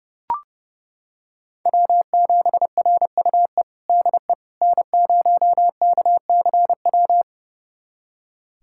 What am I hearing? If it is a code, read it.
Morse code, heard as a keyed tone, about 30 wpm, 700 Hz. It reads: W7RUE DE N0KCW
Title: USA Callsign DE Callsign - 5 Characters - 30wpm